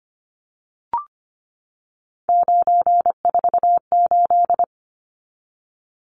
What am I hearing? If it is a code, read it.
Morse code, heard as a keyed tone, about 25 wpm, 700 Hz. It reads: 948
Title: Numbers - 3-Digit 25wpm